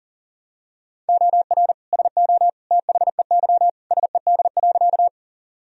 Morse code, heard as a keyed tone, about 40 wpm, 700 Hz. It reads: OR SO THEY SED.
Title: Sentences from Top 100 Words and Abbreviations 40wpm